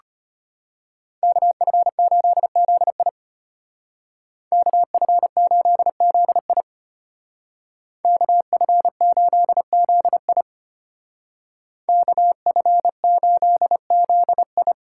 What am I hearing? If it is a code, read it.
Morse code, heard as a keyed tone, about 33 wpm, 700 Hz. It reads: KF8ZI KF8ZI KF8ZI KF8ZI